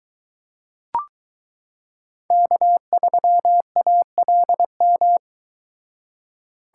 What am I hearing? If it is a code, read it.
Morse code, heard as a keyed tone, about 22 wpm, 700 Hz. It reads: K3ALM